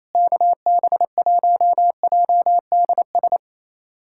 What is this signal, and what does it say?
Morse code, heard as a keyed tone, about 28 wpm, 700 Hz. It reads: KB1JDS
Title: USA Callsign DE Callsign 28wpm